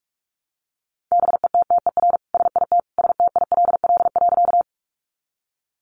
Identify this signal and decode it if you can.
Morse code, heard as a keyed tone, about 44 wpm, 700 Hz. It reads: BETTER SIT STILL.
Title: Sentences from Top 500 Words 45wpm